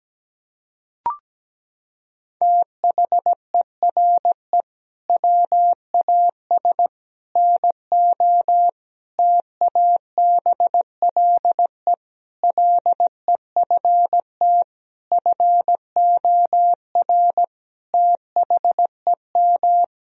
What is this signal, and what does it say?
Morse code, heard as a keyed tone, about 17 wpm, 700 Hz. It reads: THERE WAS NO TABLE LEFT FOR THEM
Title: Sentences from Top 1000 Words - Review Part 2 - 17wpm